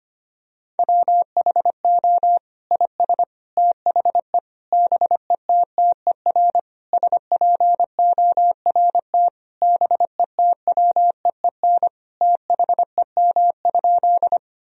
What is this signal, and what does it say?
Morse code, heard as a keyed tone, about 25 wpm, 700 Hz. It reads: WHO IS THE BETTER SPORT BETWEEN THEM?
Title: Sentences from Top 1000 Words - Review Part 4 - 25wpm